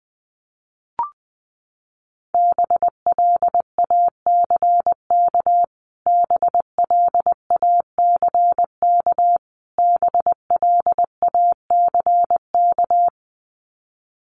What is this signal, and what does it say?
Morse code, heard as a keyed tone, about 20 wpm, 700 Hz. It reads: BLACK BLACK BLACK
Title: Top 500 Words Repeated 20wpm